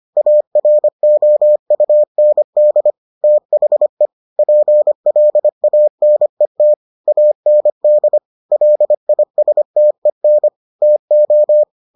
Morse code, heard as a keyed tone, about 25 wpm, 600 Hz. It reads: AROUND THE PLANET AND LISTEN TO